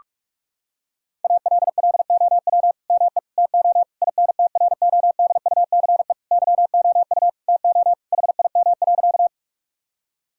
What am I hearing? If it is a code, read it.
Morse code, heard as a keyed tone, about 43 wpm, 700 Hz. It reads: ALLOW ME TO INTRODUCE YOU TO HIM.
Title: Sentences from Top 1600 Words 45wpm